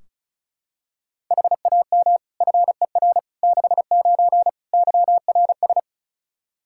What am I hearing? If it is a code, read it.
Morse code, heard as a keyed tone, about 34 wpm, 700 Hz. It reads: HAM FER 69 YRS
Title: K1USN SST QSO Practice 35wpm